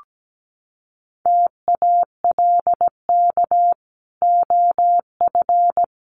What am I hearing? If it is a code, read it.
Morse code, heard as a keyed tone, about 17 wpm, 700 Hz. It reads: TALK OF
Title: Sets of 5 Words 17wpm